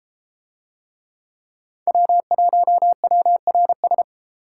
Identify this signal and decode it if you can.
Morse code, heard as a keyed tone, about 32 wpm, 700 Hz. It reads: W1WRS